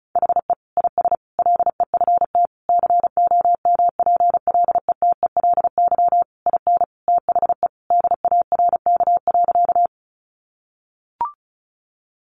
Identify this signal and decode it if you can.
Morse code, heard as a keyed tone, about 34 wpm, 700 Hz. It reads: HE IS LEFT COMPLETELY IN THE DARK.
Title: Sentences from Top 1100 Words 35wpm